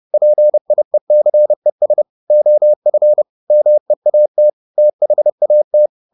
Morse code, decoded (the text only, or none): PIECES OF MEAT THAT